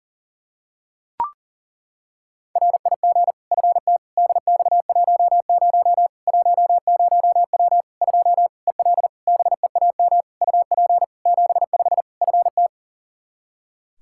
Morse code, heard as a keyed tone, about 38 wpm, 700 Hz. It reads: RIG FT DX10 10W 2 EL BEAM UP 75 FT
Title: K1USN SST QSO Practice 40wpm